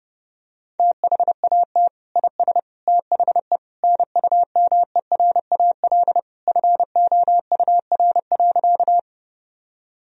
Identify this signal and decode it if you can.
Morse code, heard as a keyed tone, about 32 wpm, 700 Hz. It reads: THAT IS THE NUMERAL FOUR.
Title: Sentences from Top 1000 Words - Review Part 2 - 30wpm